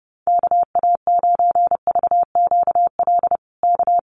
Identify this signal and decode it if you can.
Morse code, heard as a keyed tone, about 30 wpm, 700 Hz. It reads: KA9VQL K
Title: USA Call Signs Repeated 3x 30wpm